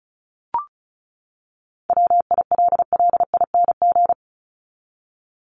Morse code, heard as a keyed tone, about 34 wpm, 700 Hz. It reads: WILLING
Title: Top 751-1000 Words 35wpm